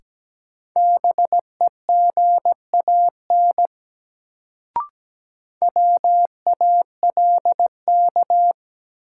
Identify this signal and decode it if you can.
Morse code, heard as a keyed tone, about 17 wpm, 700 Hz. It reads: BEGAN WALK
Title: Sentences from Top 300 Words - New Word Drill - 17wpm